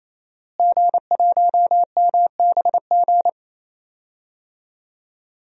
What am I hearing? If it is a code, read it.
Morse code, heard as a keyed tone, about 28 wpm, 700 Hz. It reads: G1MBG
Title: DXCC-100 Call Signs - 5 Characters - Repeated 3x - 28wpm